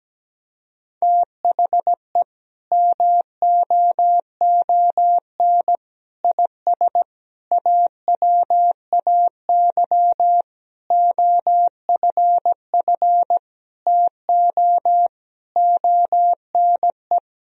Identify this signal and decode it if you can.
Morse code, heard as a keyed tone, about 17 wpm, 700 Hz. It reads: THE MOON IS AWAY OFF TO ONE
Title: Sentences from Top 1000 Words - Review Part 3 - 17wpm